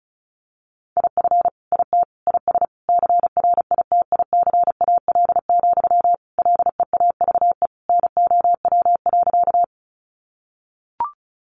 Morse code, heard as a keyed tone, about 34 wpm, 700 Hz. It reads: IF IT IS CRITICAL, LEAVE NOW.
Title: Sentences from Top 1100 Words 35wpm